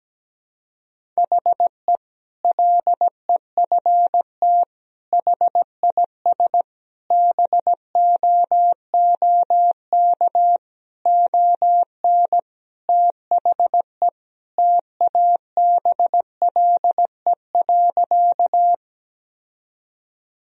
Morse code, heard as a keyed tone, about 17 wpm, 700 Hz. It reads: HE LEFT HIS BOOK ON THE TABLE.
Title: Sentences from Top 1000 Words - Review Part 2 - 17wpm